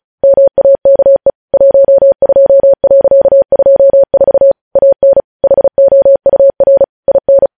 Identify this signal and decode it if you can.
Morse code, heard as a keyed tone, about 36 wpm, 550 Hz. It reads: MAKE 12.24 AN HOUR IN
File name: podcast / News headlines-1x-35x35